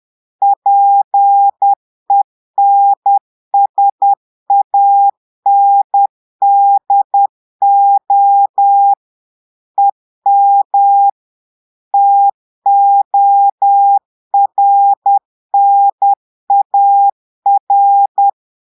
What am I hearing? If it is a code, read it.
Morse code, heard as a keyed tone, about 10 wpm, 800 Hz. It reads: PENSANDO EM TORNAR
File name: Portugais - texte en clair 1 - vitesse - 10 mots minute